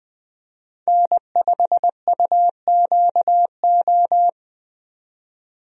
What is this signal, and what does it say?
Morse code, heard as a keyed tone, about 20 wpm, 700 Hz. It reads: N5UQO
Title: USA Call Signs - 5 Characters - 20wpm